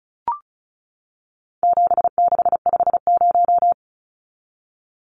Morse code, heard as a keyed tone, about 34 wpm, 700 Hz. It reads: 7650
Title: Numbers - 4-Digit 35wpm